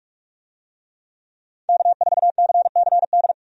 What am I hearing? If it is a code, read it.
Morse code, heard as a keyed tone, about 46 wpm, 700 Hz. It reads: K4CCD
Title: USA Call Signs - 5 Characters - 45wpm